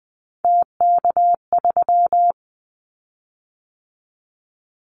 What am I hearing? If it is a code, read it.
Morse code, heard as a keyed tone, about 20 wpm, 700 Hz. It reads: TK3
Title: Callsign Prefixes for Top 100 DXCC - 20wpm